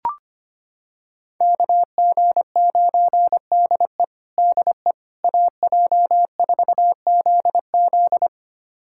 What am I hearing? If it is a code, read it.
Morse code, heard as a keyed tone, about 25 wpm, 700 Hz. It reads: KG9DE DE AJ4ZZ
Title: USA Callsign DE Callsign - 5 Characters - 25wpm